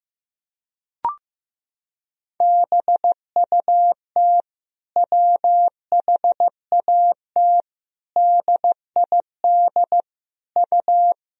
Morse code, heard as a keyed tone, about 15 wpm, 700 Hz. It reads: BUT WHAT DID U
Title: Sentences from Top 200 Words and Abbreviations 15wpm